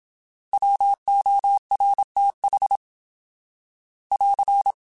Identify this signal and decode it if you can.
Morse code, heard as a keyed tone, about 26 wpm, 800 Hz. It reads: WORTH <AR>
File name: Alphabetical / Alphabetical mp3s / Ws @25